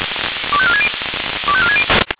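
Five-tone selective call (ZVEI), heard as quick runs of five tones, tones of about 70 ms. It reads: 25460 25460
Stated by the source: Five-tone ZVEI selective call 25418, received by SDR.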